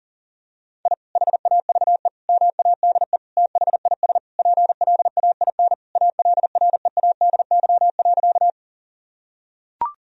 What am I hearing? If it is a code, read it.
Morse code, heard as a keyed tone, about 40 wpm, 700 Hz. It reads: I HAVE MADE THIS PLAIN ALREADY.
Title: Sentences from Top 1000 Words - Review Part 2 - 40wpm